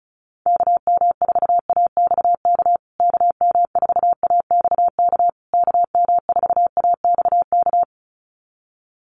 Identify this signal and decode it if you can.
Morse code, heard as a keyed tone, about 34 wpm, 700 Hz. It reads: KM4AXK KM4AXK KM4AXK
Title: USA Call Signs Repeated 3x 35wpm